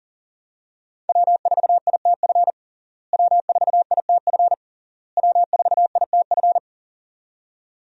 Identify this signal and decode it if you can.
Morse code, heard as a keyed tone, about 40 wpm, 700 Hz. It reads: W4ITF W4ITF W4ITF
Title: USA Call Signs - 5 Characters - Repeated 3x - 40wpm